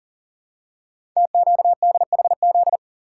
Morse code, heard as a keyed tone, about 38 wpm, 700 Hz. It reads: TQDHZ
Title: Code Groups - 5 Letter:Numbers - 40wpm